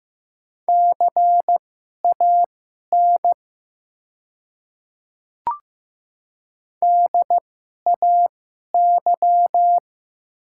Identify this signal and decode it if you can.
Morse code, heard as a keyed tone, about 15 wpm, spaced as for 11 wpm, 700 Hz. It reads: CAN DAY
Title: 2x Character Spacing - Top 200 Words - 15wpm